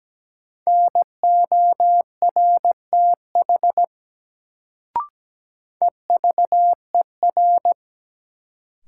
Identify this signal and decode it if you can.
Morse code, heard as a keyed tone, about 17 wpm, 700 Hz. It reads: NORTH EVER
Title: Sentences from Top 400 Words - New Word Drill - 17wpm